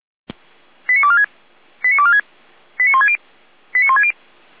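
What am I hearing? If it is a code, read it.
Five-tone selective call (ZVEI), heard as quick runs of five tones, tones of about 80 ms. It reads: 89246 89246 89160 89160
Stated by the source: ZVEI five-tone call received on an SDR.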